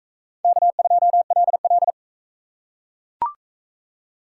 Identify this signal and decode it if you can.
Morse code, heard as a keyed tone, about 38 wpm, 700 Hz. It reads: K2LL